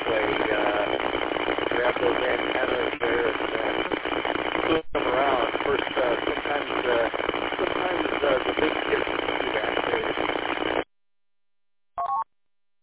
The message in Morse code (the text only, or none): none